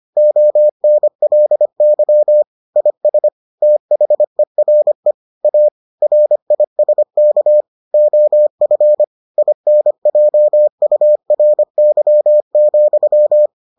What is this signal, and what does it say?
Morse code, heard as a keyed tone, about 25 wpm, 600 Hz. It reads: ONLY IS THERE A RISK OF INJURY,